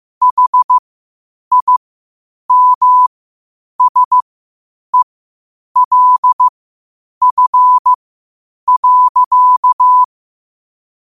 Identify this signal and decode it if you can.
Morse code, heard as a keyed tone, about 15 wpm, spaced as for 8 wpm, 1000 Hz. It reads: HIMSELF.
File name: qotdfiles / 8wpm